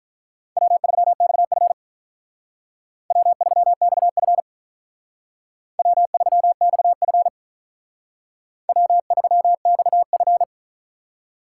Morse code, heard as a keyed tone, about 44 wpm, 700 Hz. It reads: W3XF W3XF W3XF W3XF